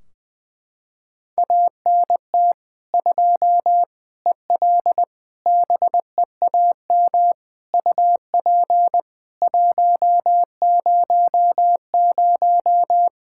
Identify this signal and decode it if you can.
Morse code, heard as a keyed tone, about 20 wpm, 700 Hz. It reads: ANT 2 EL BEAM UP 100
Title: Practice QSO - Exchange 2 - Info - 20wpm